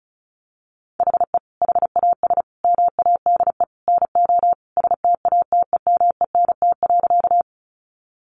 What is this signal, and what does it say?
Morse code, heard as a keyed tone, about 34 wpm, 700 Hz. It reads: HE HAS MADE NO STATEMENT.
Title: Sentences from Top 1200 Words 35wpm